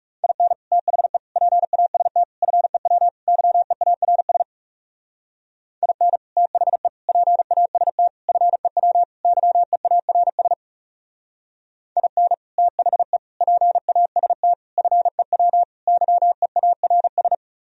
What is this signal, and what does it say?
Morse code, heard as a keyed tone, about 37 wpm, 700 Hz. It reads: IN THE PAST FEW YEARS IN THE PAST FEW YEARS IN THE PAST FEW YEARS
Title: Speed-Racing - Sets of 5 Words 30wpm